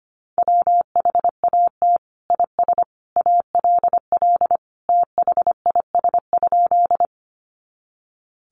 Morse code, heard as a keyed tone, about 25 wpm, 700 Hz. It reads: WHAT IS ALL THIS?
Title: Sentences from Top 100 Words 25wpm